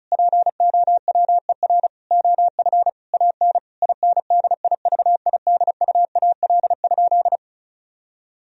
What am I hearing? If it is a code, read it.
Morse code, heard as a keyed tone, about 34 wpm, 700 Hz. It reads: POWER OF AN INDIVIDUAL?
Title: Sentences from Top 1000 Words - Review Part 4 - 35wpm